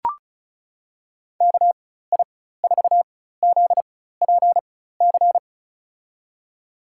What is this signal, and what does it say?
Morse code, heard as a keyed tone, about 36 wpm, spaced as for 16 wpm, 700 Hz. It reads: KI4ZPC
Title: USA Call Signs - 6 Characters - 4x Character Spacing - 35wpm